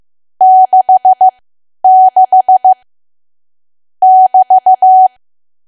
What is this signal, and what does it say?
Morse code, heard as a keyed tone, about 15 wpm, spaced as for 10 wpm, 750 Hz. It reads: 66 <BT>
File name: Archive / 10 WPM / 10WPM